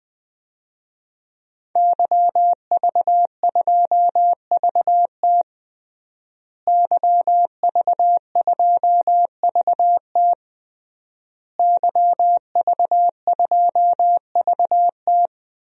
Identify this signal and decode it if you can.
Morse code, heard as a keyed tone, about 21 wpm, 700 Hz. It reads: YV2VT YV2VT YV2VT
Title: DXCC-100 Call Signs - 5 Characters - Repeated 3x - 20wpm